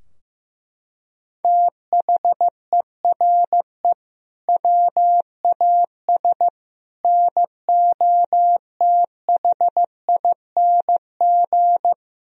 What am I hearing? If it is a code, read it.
Morse code, heard as a keyed tone, about 15 wpm, 700 Hz. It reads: THERE WAS NOTHING
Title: Sentences from Top 1000 Words - Review Part 4 - 15wpm